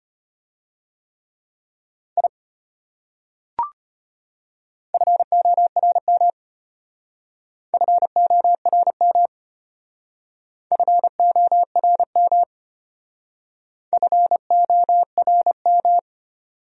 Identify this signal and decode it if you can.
Morse code, heard as a keyed tone, about 32 wpm, 700 Hz. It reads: I FORM FORM FORM FORM